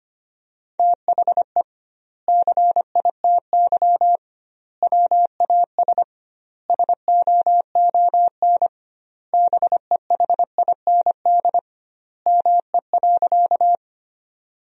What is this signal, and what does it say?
Morse code, heard as a keyed tone, about 25 wpm, 700 Hz. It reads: THE CITY WAS SOON BEHIND ME.
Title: 2x Word-Spacing - Sentences from Top 500 Words - 25wpm